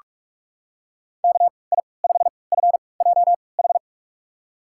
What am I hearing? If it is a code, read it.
Morse code, heard as a keyed tone, about 41 wpm, spaced as for 25 wpm, 700 Hz. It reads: KI5FJH
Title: USA Call Signs - 6 Characters - 3x Character Spacing - 45wpm